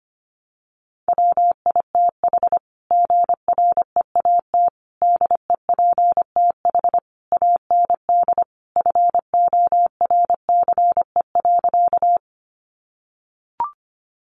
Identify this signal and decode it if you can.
Morse code, heard as a keyed tone, about 25 wpm, 700 Hz. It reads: WITH GREAT DEPTH AND FORCE.
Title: Sentences from Top 1200 Words 25wpm